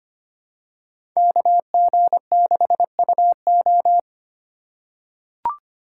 Morse code, heard as a keyed tone, about 25 wpm, 700 Hz. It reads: KG6UO
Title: USA Call Signs - 2x Character Spacing - 25wpm